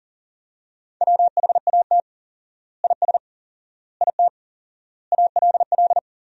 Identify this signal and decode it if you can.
Morse code, heard as a keyed tone, about 40 wpm, 700 Hz. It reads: WHAT IS IT ALL
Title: 4x Word-Spacing - Sentences from Top 100 Words - 40wpm